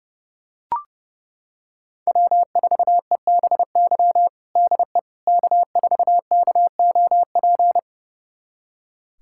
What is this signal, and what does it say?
Morse code, heard as a keyed tone, about 30 wpm, 700 Hz. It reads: W4EBY DE K4KOP
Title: USA Callsign DE Callsign - 5 Characters - 30wpm